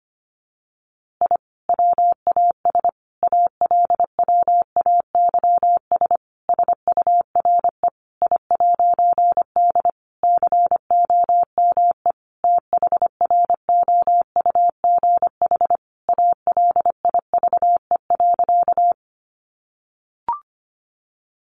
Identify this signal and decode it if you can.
Morse code, heard as a keyed tone, about 25 wpm, 700 Hz. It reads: I WAS ALWAYS SURE I'D COME THROUGH ALIVE.
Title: A Sound Of Thunder 25wpm